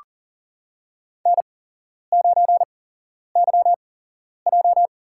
Morse code, heard as a keyed tone, about 40 wpm, spaced as for 11 wpm, 700 Hz. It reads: N9YJ
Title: USA Call Signs - 4 Characters - 8x Character Spacing - 40wpm